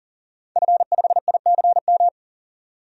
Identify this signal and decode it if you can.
Morse code, heard as a keyed tone, about 40 wpm, 700 Hz. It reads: F5ICM
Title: DXCC-100 Call Signs - 5 Characters - 40wpm